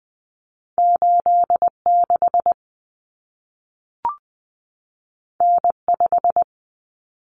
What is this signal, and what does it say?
Morse code, heard as a keyed tone, about 20 wpm, 700 Hz. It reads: 86 N5
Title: Common Cut Numbers - 2-Digit 20wpm